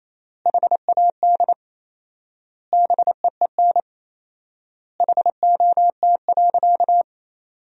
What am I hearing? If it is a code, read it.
Morse code, heard as a keyed tone, about 28 wpm, 700 Hz. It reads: HAD BEEN HOT.
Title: 4x Word-Spacing - Sentences from Top 500 Words - 28wpm